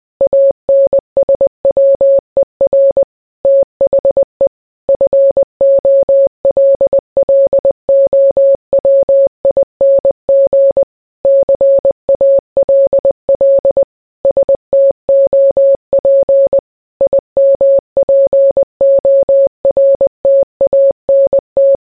Morse code, heard as a keyed tone, about 20 wpm, 550 Hz. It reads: ANSWER THE FOLLOWING CALL STOP IMPORTANT